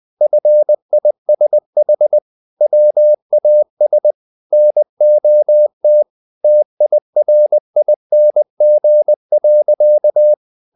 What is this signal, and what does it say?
Morse code, heard as a keyed tone, about 20 wpm, 600 Hz. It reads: FISH WAS NOT TIRING.
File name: the old man and the sea / the old man and the sea part 1 20wpm